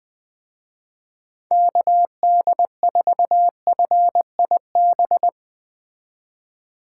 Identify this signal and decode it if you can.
Morse code, heard as a keyed tone, about 21 wpm, 700 Hz. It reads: KD4FIB